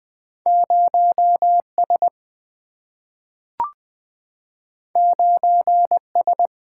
Morse code, heard as a keyed tone, about 20 wpm, 700 Hz. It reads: 0S 9S
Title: Single-Letter Single-Number 20wpm